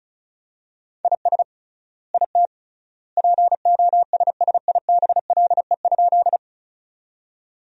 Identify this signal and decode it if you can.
Morse code, heard as a keyed tone, about 34 wpm, 700 Hz. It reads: IS IT POSSIBLE?
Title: 3x Word-Spacing - Sentences from Top 500 Words - 35wpm